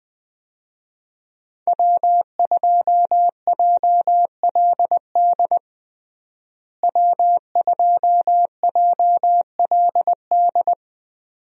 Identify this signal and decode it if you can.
Morse code, heard as a keyed tone, about 20 wpm, 700 Hz. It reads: W2JLD W2JLD
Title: USA Call Signs - 5 Characters - Repeated 2x - 20wpm